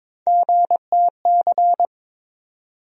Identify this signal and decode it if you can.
Morse code, heard as a keyed tone, about 21 wpm, 700 Hz. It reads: GTC
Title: USA Call Sign Postfixes - 3 Letters 22wpm